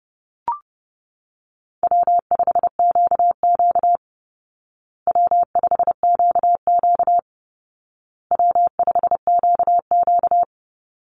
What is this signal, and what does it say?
Morse code, heard as a keyed tone, about 30 wpm, 700 Hz. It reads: W5QQ W5QQ W5QQ
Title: Call Signs Repeated 2x - 30wpm